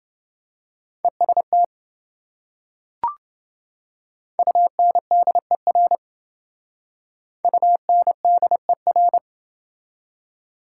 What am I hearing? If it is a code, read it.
Morse code, heard as a keyed tone, about 28 wpm, 700 Hz. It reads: EST UNDER UNDER